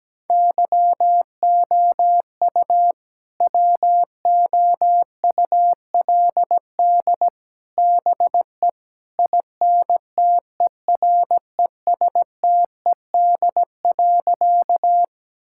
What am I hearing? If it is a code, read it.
Morse code, heard as a keyed tone, about 17 wpm, 700 Hz. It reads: YOU WOULD BE INTERESTED.
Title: Sentences from Top 1000 Words - Review Part 6 - 17wpm